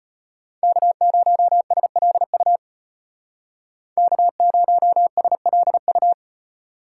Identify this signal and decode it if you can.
Morse code, heard as a keyed tone, about 30 wpm, 700 Hz. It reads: K0SLU K0SLU
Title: Speed Racing - USA Call Signs 25wpm